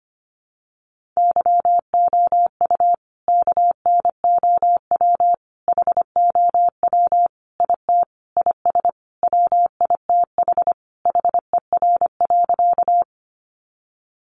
Morse code, heard as a keyed tone, about 25 wpm, 700 Hz. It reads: YOU KNOW HOW IT IS WITH HER.
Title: Sentences from Top 100 Words 25wpm